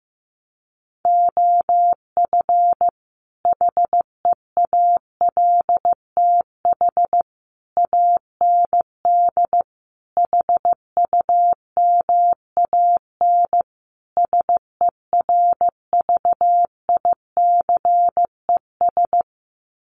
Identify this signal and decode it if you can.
Morse code, heard as a keyed tone, about 14 wpm, 700 Hz. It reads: OF HEALTH AND HUMAN SERVICES
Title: Sets of 5 Words 15wpm